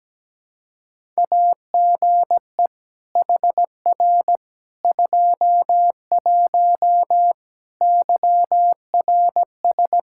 Morse code, heard as a keyed tone, about 17 wpm, 700 Hz. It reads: AGE HR 21 YRS
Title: K1USN SST QSO Practice 17wpm